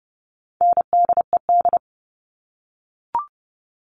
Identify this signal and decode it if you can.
Morse code, heard as a keyed tone, about 30 wpm, 700 Hz. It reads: NDED
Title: All Cut Numbers - 4-Digit 30wpm